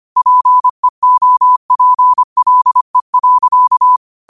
Morse code, heard as a keyed tone, about 25 wpm, 1000 Hz. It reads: PEOPLE.